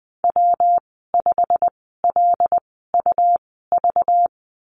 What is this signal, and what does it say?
Morse code, heard as a keyed tone, about 20 wpm, spaced as for 15 wpm, 700 Hz. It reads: W5LUV
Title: 213 Lesson 34g - Number 8 - Call Signs 20wpm